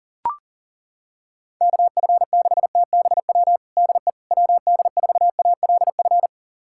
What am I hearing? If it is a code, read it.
Morse code, heard as a keyed tone, about 40 wpm, 700 Hz. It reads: KF6TBW DE WD4ALF
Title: USA Callsign DE Callsign - 6 Characters - 40wpm